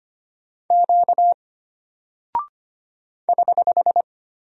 Morse code, heard as a keyed tone, about 25 wpm, 700 Hz. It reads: Q <HH>
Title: Single Character (Letter:Number:Punctuation:Prosign) Alt 25wpm